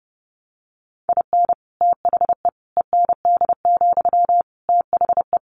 Morse code, heard as a keyed tone, about 30 wpm, 700 Hz. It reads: IN THE END, THE
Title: Additional Easy Phrases 30wpm